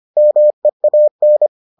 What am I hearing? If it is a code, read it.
Morse code, heard as a keyed tone, about 25 wpm, 600 Hz. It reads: MEAN